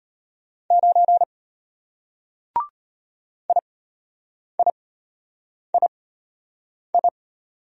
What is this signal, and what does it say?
Morse code, heard as a keyed tone, about 33 wpm, spaced as for 8 wpm, 700 Hz. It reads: 9 IIII